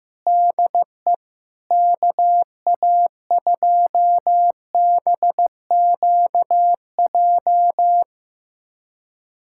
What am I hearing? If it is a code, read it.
Morse code, heard as a keyed tone, about 15 wpm, 700 Hz. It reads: DE KA2BQJ
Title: USA Callsign DE Callsign - 6 Characters - 15wpm